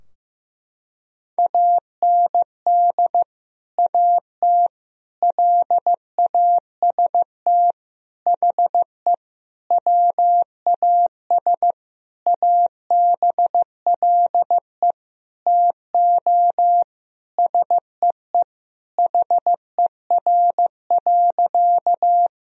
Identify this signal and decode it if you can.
Morse code, heard as a keyed tone, about 16 wpm, 700 Hz. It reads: AND AT LAST HE WAS ABLE TO SEE HER.
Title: Sentences from Top 1000 Words - Review Part 2 - 15wpm